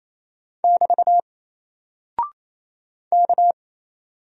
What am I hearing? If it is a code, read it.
Morse code, heard as a keyed tone, about 27 wpm, 700 Hz. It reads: <BT> K
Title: Single Character (Letter:Number:Punctuation:Prosign) 28wpm